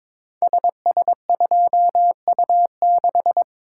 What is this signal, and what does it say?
Morse code, heard as a keyed tone, about 22 wpm, 690 Hz. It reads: SS2U6
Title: Code Groups - 5 Letter:Numbers - 22wpm